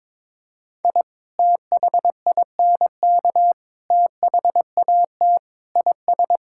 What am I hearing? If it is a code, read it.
Morse code, heard as a keyed tone, about 22 wpm, 700 Hz. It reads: I THINK THAT IS
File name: practice / 22wpm / Sentences with 3-Word Phrases - 22wpm